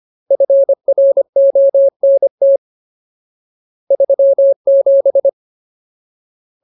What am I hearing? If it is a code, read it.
Morse code, heard as a keyed tone, about 25 wpm, 550 Hz. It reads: FRONT 37
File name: Ultimate Word List / 25WPM Ultimate